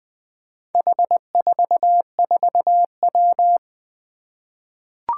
Morse code, heard as a keyed tone, about 20 wpm, 700 Hz. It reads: H44W
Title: Four Challenging Letter-Number Combinations 20wpm